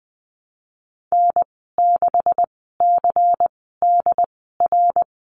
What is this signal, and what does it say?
Morse code, heard as a keyed tone, about 19 wpm, spaced as for 14 wpm, 700 Hz. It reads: N6CDR
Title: 147 Lesson 24g - Letter y - Call Signs 20wpm